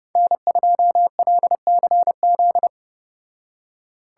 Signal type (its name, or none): Morse code